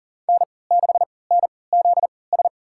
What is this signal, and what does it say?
Morse code, heard as a keyed tone, about 40 wpm, spaced as for 23 wpm, 700 Hz. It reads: N6NZS
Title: USA Call Signs - 5 Characters - 3x Character Spacing - 40wpm